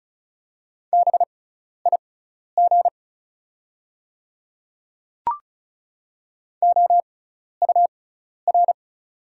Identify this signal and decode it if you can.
Morse code, heard as a keyed tone, about 35 wpm, spaced as for 12 wpm, 690 Hz. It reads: BIG OUR
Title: 6x Character Spacing - Top 200 Words - 35wpm